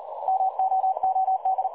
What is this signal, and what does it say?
Morse code, heard as a keyed tone, about 38 wpm, 760 Hz. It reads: MOOD
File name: dlazaawansowanych / nagraniazpasma / websdr recording 3565.0kHz